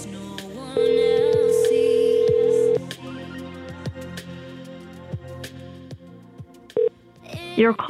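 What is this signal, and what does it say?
Ringback tone (North America), 440 + 480 Hz.
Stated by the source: Prank phone calls made on a show streamed live on YouTube.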